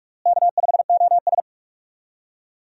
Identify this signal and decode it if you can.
Morse code, heard as a keyed tone, about 46 wpm, 700 Hz. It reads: K5OS
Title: USA Call Signs - 4 Characters - Repeated 3x - 45wpm